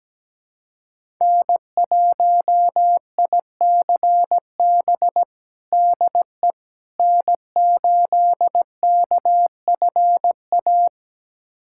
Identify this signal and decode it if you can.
Morse code, heard as a keyed tone, about 17 wpm, 700 Hz. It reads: N1ICB DE N8KFA
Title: USA Callsign DE Callsign - 5 Characters - 17wpm